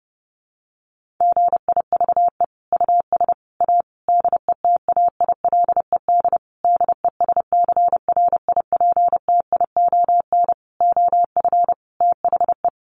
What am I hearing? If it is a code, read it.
Morse code, heard as a keyed tone, about 30 wpm, 700 Hz. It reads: GIVE US A DETAILED DESCRIPTION OF THE
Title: Sentences from Top 1500 Words 30wpm